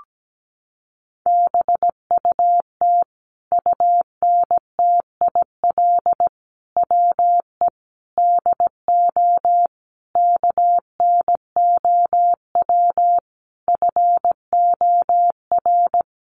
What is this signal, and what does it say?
Morse code, heard as a keyed tone, about 17 wpm, 700 Hz. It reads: BUT UNTIL WE DO KNOW FOR
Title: A Sound Of Thunder 17wpm